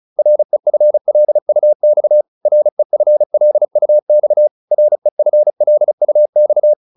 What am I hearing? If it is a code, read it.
Morse code, heard as a keyed tone, about 34 wpm, 600 Hz. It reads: REFLUX REFLUX REFLUX
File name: RE6 35